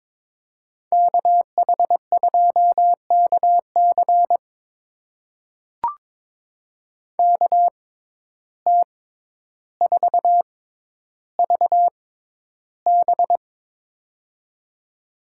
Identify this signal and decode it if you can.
Morse code, heard as a keyed tone, about 22 wpm, spaced as for 11 wpm, 700 Hz. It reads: KH2KC KT4VB
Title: USA Call Signs - 6x Character Spacing - 22wpm